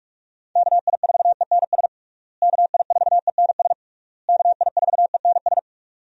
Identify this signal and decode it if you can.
Morse code, heard as a keyed tone, about 41 wpm, 700 Hz. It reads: KI4ENS KI4ENS KI4ENS
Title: USA Call Signs - 6 Characters - Repeated 3x - 45wpm